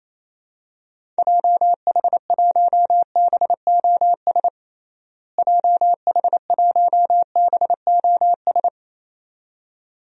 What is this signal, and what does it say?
Morse code, heard as a keyed tone, about 28 wpm, 700 Hz. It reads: JH1BOS JH1BOS
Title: DXCC-100 Call Signs - 6 Characters - Repeated 2x - 28wpm